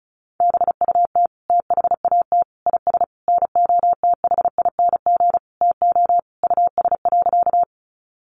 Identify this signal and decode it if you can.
Morse code, heard as a keyed tone, about 34 wpm, 700 Hz. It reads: BUT THAT IS NOTHING TO US.
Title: Sentences from Top 500 Words 35wpm